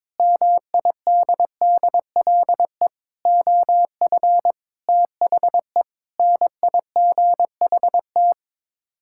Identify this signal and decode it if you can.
Morse code, heard as a keyed tone, about 21 wpm, 700 Hz. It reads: MIDDLE OF THE NIGHT
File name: practice / 22wpm / Build A Phrase - 5-Word Phrases - 22wpm